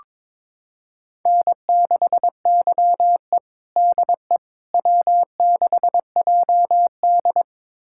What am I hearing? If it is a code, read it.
Morse code, heard as a keyed tone, about 22 wpm, 700 Hz. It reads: N6YE DE W6JD
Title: USA Callsign DE Callsign - 4 Characters - 22wpm